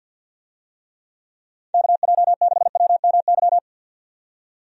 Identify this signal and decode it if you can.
Morse code, heard as a keyed tone, about 50 wpm, 700 Hz. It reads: KJ6WMY
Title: USA Call Signs - 6 Characters - 50wpm